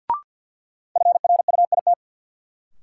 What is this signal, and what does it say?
Morse code, heard as a keyed tone, about 51 wpm, 700 Hz. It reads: FRUIT
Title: Sentences from Top 1300 Words - New Word Drill Alt - 50wpm